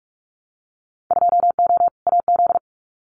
Morse code, heard as a keyed tone, about 44 wpm, 700 Hz. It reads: 2O AZ
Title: Winter Field Day Exchanges 45wpm